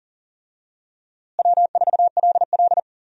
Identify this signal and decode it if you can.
Morse code, heard as a keyed tone, about 38 wpm, 700 Hz. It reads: W4LL
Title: USA Call Signs - 4 Characters - Repeated 3x - 40wpm